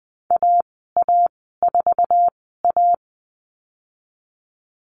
Morse code, heard as a keyed tone, about 20 wpm, spaced as for 15 wpm, 700 Hz. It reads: AA4A
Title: 055 Lesson 10g - Letter r - Call Signs 20wpm